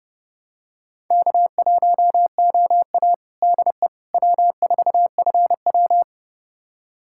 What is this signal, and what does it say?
Morse code, heard as a keyed tone, about 29 wpm, 700 Hz. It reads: K1OA DE W4FW
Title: USA Callsign DE Callsign - 4 Characters - 30wpm